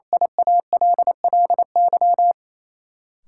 Morse code, heard as a keyed tone, about 28 wpm, 700 Hz. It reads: IALLY
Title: Sentences from Top 1300 Words - New Word Drill Alt - 28wpm